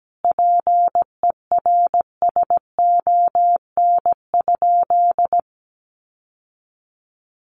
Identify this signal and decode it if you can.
Morse code, heard as a keyed tone, about 17 wpm, 700 Hz. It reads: PERSON?
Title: Sentences from Top 1400 Words 17wpm